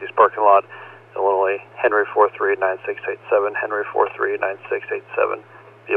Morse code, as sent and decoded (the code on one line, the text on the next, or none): none